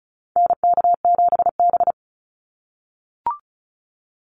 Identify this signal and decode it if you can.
Morse code, heard as a keyed tone, about 34 wpm, 700 Hz. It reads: NK7B
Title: Call Signs - 35wpm